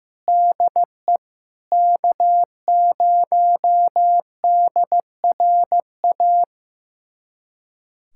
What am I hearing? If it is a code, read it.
Morse code, heard as a keyed tone, about 16 wpm, 700 Hz. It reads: DE K0DRA
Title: USA Callsign DE Callsign - 5 Characters - 15wpm